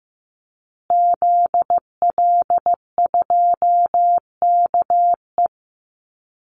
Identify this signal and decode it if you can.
Morse code, heard as a keyed tone, about 14 wpm, 700 Hz. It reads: ZL2KE
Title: Call Signs Repeated 2x - 15wpm